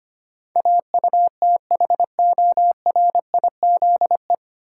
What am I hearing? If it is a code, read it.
Morse code, heard as a keyed tone, about 25 wpm, 700 Hz. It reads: AUTHORIZE
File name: practice / 25wpm / 9 Letter Words - First 3 Letters Given - 25wpm